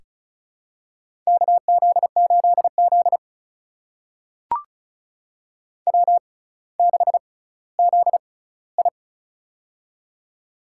Morse code, heard as a keyed tone, about 36 wpm, spaced as for 18 wpm, 700 Hz. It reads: KZ8Z W6ZI
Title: USA Call Signs - 4 Characters - 6x Character Spacing - 35wpm